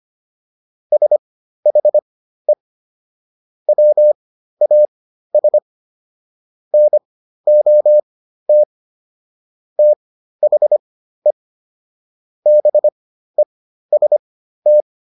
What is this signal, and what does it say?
Morse code, heard as a keyed tone, about 25 wpm, spaced as for 13 wpm, 600 Hz. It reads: SHE WAS NOT THE BEST